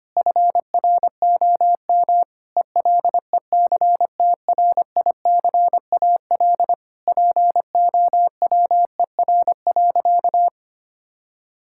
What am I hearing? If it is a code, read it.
Morse code, heard as a keyed tone, about 25 wpm, 700 Hz. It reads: FROM ELECTRICAL POWER.
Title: Sentences from Top 1000 Words - Review Part 6 - 25wpm